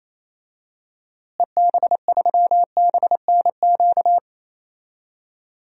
Morse code, heard as a keyed tone, about 28 wpm, 700 Hz. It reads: EB3BNQ
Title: DXCC-100 Call Signs - 6 Characters - Repeated 2x - 28wpm